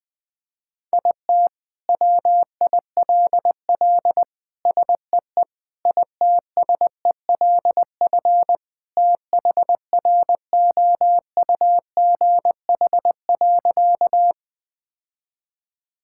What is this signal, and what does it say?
Morse code, heard as a keyed tone, about 20 wpm, 700 Hz. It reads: IT WILL SEE ITSELF THROUGH.
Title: Sentences from Top 1000 Words - Review Part 5 - 20wpm